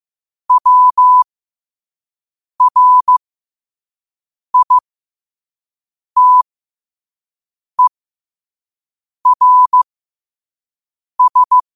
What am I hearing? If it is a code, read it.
Morse code, heard as a keyed tone, about 15 wpm, spaced as for 5 wpm, 1000 Hz. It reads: WRITERS
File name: qotdfiles / 5wpm